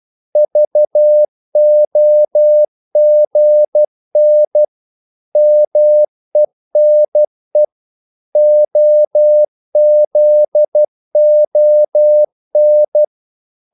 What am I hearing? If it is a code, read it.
Morse code, heard as a keyed tone, about 12 wpm, 600 Hz. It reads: VOGN MENE OZON
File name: Korte ord DK 12wpm